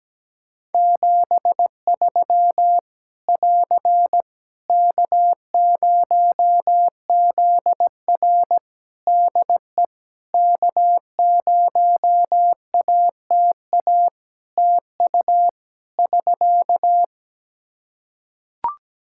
Morse code, heard as a keyed tone, about 17 wpm, 700 Hz. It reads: 73 <AR> K0ZR DE K0ATA TU <SK>
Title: Practice QSO - Exchange 4 - Ending - 17wpm